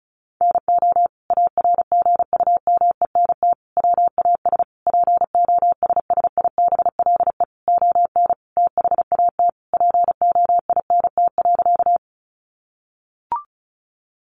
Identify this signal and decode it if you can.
Morse code, heard as a keyed tone, about 34 wpm, 700 Hz. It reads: NO ARGUMENT WAS POSSIBLE ON THAT POINT.
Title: Sentences from Top 1400 Words 35wpm